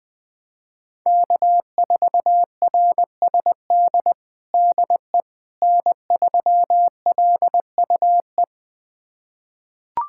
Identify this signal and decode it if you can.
Morse code, heard as a keyed tone, about 21 wpm, 700 Hz. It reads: K4RSD DE N3LUE
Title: USA Callsign DE Callsign - 5 Characters - 20wpm